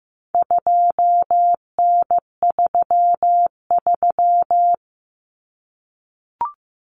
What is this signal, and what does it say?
Morse code, heard as a keyed tone, about 15 wpm, 700 Hz. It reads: 2N33
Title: Common Cut Numbers - 4-Digit 15wpm